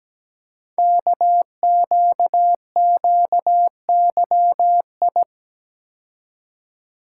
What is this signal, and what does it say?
Morse code, heard as a keyed tone, about 17 wpm, 700 Hz. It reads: KQQYI
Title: Code Groups - 5 Letter:Numbers - 17wpm